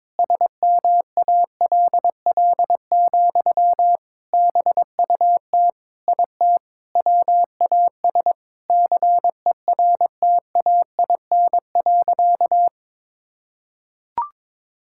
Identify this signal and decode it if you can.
Morse code, heard as a keyed tone, about 22 wpm, 700 Hz. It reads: SMALL, BUT IT WAS CERTAIN.
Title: Sentences from Top 1000 Words - Review Part 6 - 22wpm